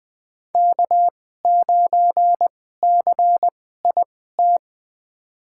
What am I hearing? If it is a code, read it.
Morse code, heard as a keyed tone, about 20 wpm, spaced as for 15 wpm, 700 Hz. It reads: K9CIT
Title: USA Call Signs - 5 Characters - 2x Character Spacing - 20wpm